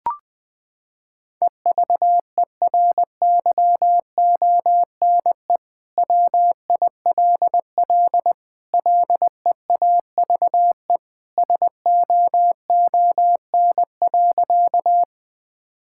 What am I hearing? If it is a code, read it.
Morse code, heard as a keyed tone, about 20 wpm, 700 Hz. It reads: EVERYONE WILL LEAVE SOON.
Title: Sentences from Top 1000 Words - Review Part 5 - 20wpm